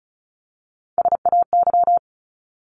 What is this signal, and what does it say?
Morse code, heard as a keyed tone, about 34 wpm, 700 Hz. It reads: SAY
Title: Top 200 Words 35wpm